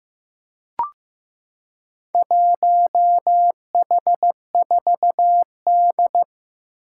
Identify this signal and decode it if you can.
Morse code, heard as a keyed tone, about 15 wpm, 700 Hz. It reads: 1H4D
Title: Four Challenging Letter-Number Combinations 15wpm